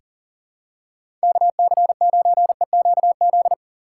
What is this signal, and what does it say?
Morse code, heard as a keyed tone, about 38 wpm, 700 Hz. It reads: KC9EQZ
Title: USA Call Signs - 6 Characters - 4x Character Spacing - 40wpm